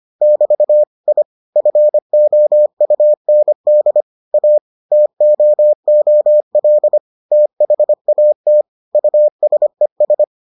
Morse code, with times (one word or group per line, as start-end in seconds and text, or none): 0.2-0.8 <BT>
1.1-1.2 I
1.6-4.0 FOUND
4.3-4.6 A
4.9-7.0 TOOL
7.3-8.6 THAT
8.9-10.2 USES